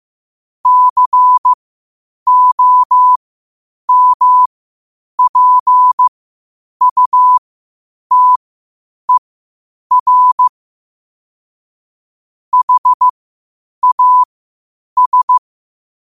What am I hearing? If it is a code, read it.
Morse code, heard as a keyed tone, about 15 wpm, spaced as for 8 wpm, 1000 Hz. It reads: COMPUTER HAS
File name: qotdfiles / 8wpm